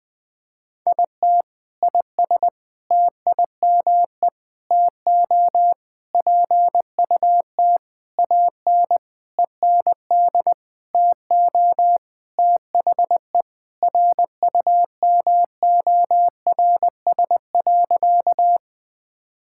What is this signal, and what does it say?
Morse code, heard as a keyed tone, about 21 wpm, 700 Hz. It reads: IT IS TIME TO PUT AN END TO THE RUMORS.
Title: Sentences with 6-Word Phrases - Spoken Beginning - 20wpm